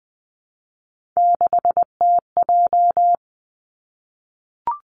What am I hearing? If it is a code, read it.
Morse code, heard as a keyed tone, about 20 wpm, 700 Hz. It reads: 6TJ
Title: Three Challenging Letter-Number Combinations 20wpm